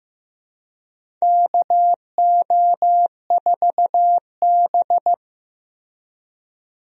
Morse code, heard as a keyed tone, about 15 wpm, 700 Hz. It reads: KO4B
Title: USA Call Signs - 4 Characters - Repeated 3x - 15wpm